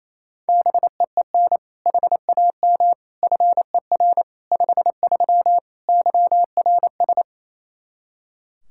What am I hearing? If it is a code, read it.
Morse code, heard as a keyed tone, about 28 wpm, 700 Hz. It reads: BEEN HAM FER 53 YRS
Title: Practice QSO - Exchange 3 - More Info - 28wpm